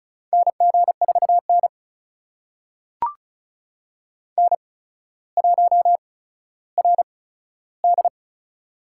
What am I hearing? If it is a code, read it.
Morse code, heard as a keyed tone, about 36 wpm, spaced as for 15 wpm, 700 Hz. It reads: NG4N N1RD
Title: USA Call Signs - 4 Characters - 8x Character Spacing - 35wpm